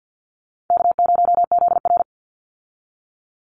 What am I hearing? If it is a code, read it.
Morse code, heard as a keyed tone, about 50 wpm, 700 Hz. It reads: K0ZR
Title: Call Signs Repeated 3x - 50wpm